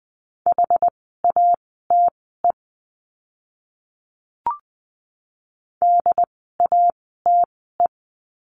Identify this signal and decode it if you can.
Morse code, heard as a keyed tone, about 20 wpm, spaced as for 14 wpm, 700 Hz. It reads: HATE DATE
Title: 068 Lesson 12f - Letter d - Words 20wpm